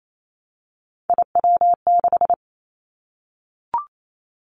Morse code, heard as a keyed tone, about 28 wpm, 700 Hz. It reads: IW6
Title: Three Challenging Letter-Number Combinations 28wpm